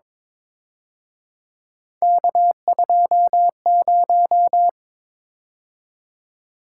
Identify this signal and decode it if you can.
Morse code, heard as a keyed tone, about 25 wpm, 700 Hz. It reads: K20
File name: Speed-Racing - Three Letter-Number Combinations - 22wpm